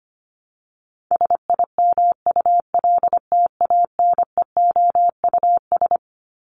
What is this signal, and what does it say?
Morse code, heard as a keyed tone, about 25 wpm, 700 Hz. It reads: SIMULTANEOUS
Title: LY Suffix - Top 100 Words with Base Words 25wpm